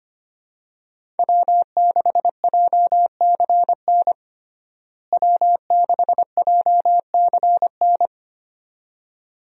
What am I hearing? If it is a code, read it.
Morse code, heard as a keyed tone, about 25 wpm, 690 Hz. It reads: W6JCN W6JCN
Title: USA Call Signs - 5 Characters - Repeated 2x - 25wpm